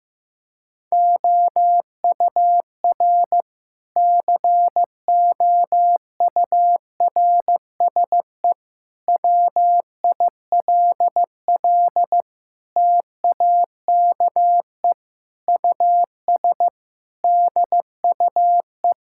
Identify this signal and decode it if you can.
Morse code, heard as a keyed tone, about 14 wpm, 700 Hz. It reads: OUR COURSE WILL TAKE US DUE
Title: Sentences from Top 1000 Words - Review Part 6 - 15wpm